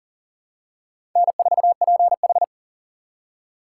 Morse code, heard as a keyed tone, about 40 wpm, 700 Hz. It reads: N4PH
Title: USA Call Signs - 4 Characters - 2x Character Spacing - 40wpm